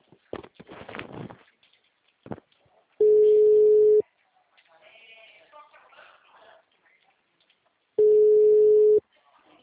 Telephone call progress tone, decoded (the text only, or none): ringback tone